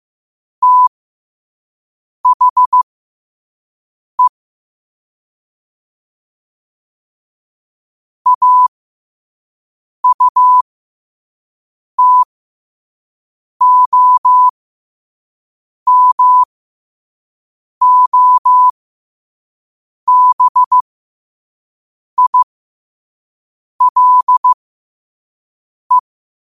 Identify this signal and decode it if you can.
Morse code, heard as a keyed tone, about 15 wpm, spaced as for 5 wpm, 1000 Hz. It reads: THE AUTOMOBILE
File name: qotdfiles / 5wpm